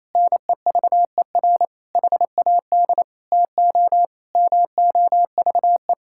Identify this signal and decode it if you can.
Morse code, heard as a keyed tone, about 28 wpm, 700 Hz. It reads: NEVER HAD TO MOVE
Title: Sentences from Top 1600 Words 28wpm